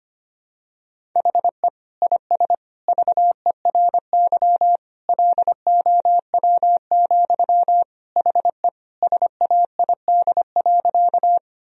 Morse code, heard as a keyed tone, about 25 wpm, 700 Hz. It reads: HE IS VERY LOW, HE SAID.
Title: Sentences from Top 1000 Words - Review Part 1 - 25wpm